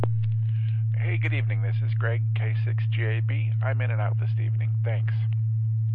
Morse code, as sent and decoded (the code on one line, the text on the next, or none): none